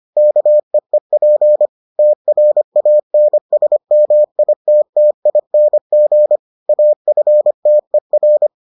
Morse code, heard as a keyed tone, about 25 wpm, 600 Hz. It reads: KEEP TRANSMITTING AFTER